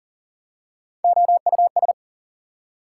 Morse code, heard as a keyed tone, about 40 wpm, 700 Hz. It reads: OUS
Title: Prefixes and Suffixes 40wpm